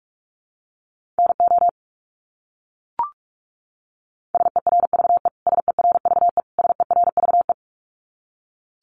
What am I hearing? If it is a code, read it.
Morse code, heard as a keyed tone, about 44 wpm, 700 Hz. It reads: NO SERVE SERVE SERVE
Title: Top 500 Words Repeated 45wpm